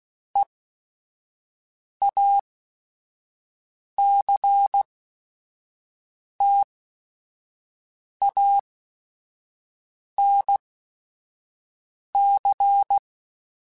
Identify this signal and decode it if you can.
Morse code, heard as a keyed tone, about 16 wpm, spaced as for 5 wpm, 800 Hz. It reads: EACTANC
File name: Technical Subjects / Technical Subjects-mp3s / XC @5